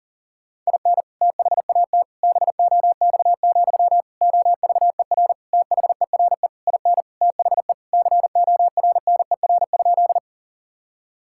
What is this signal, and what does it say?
Morse code, heard as a keyed tone, about 40 wpm, 700 Hz. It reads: IN THAT BOX, OVER THERE IN THE CORNER?
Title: Sentences from Top 1600 Words 40wpm